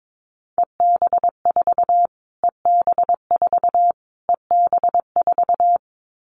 Morse code, heard as a keyed tone, about 22 wpm, 700 Hz. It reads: EB4 EB4 EB4
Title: Callsign Prefixes for Top 25 DXCC - Repeated 3x - 22wpm